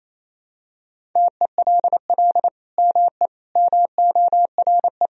Morse code, heard as a keyed tone, about 28 wpm, 700 Hz. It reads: TELL ME MORE
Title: Sentences from Top 1000 Words - Review Part 4 - 28wpm